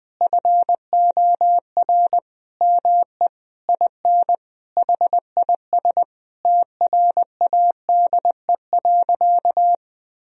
Morse code, heard as a keyed tone, about 20 wpm, 700 Hz. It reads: FOR ME IN HIS TRADE.
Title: Sentences from Top 1000 Words - Review Part 4 - 20wpm